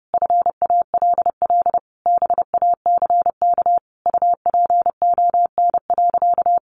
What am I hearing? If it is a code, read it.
Morse code, heard as a keyed tone, about 30 wpm, 700 Hz. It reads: FALL BACK UPON.
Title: Sentences from Top 1100 Words 30wpm